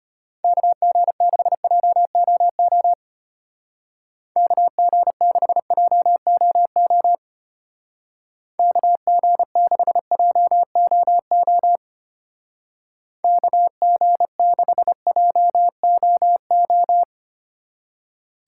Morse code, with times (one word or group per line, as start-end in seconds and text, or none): 0.4-2.9 KG6JOO
4.4-7.2 KG6JOO
8.6-11.8 KG6JOO
13.2-17.0 KG6JOO